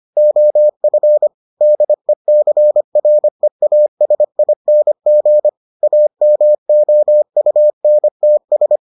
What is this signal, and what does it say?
Morse code, heard as a keyed tone, about 25 wpm, 600 Hz. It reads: OF DECREASING AMOUNTS